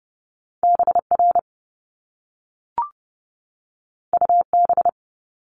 Morse code, heard as a keyed tone, about 30 wpm, 700 Hz. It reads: BR UB
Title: Top Two Letter Combinations in Words 30wpm